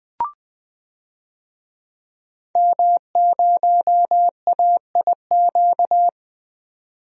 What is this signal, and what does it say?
Morse code, heard as a keyed tone, about 20 wpm, 700 Hz. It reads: M0AIQ
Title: DXCC-100 Call Signs - 5 Characters - 20wpm